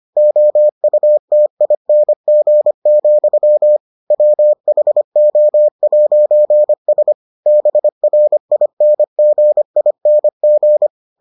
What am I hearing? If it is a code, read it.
Morse code, heard as a keyed tone, about 25 wpm, 600 Hz. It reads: OUTING, WHO'S BRINGING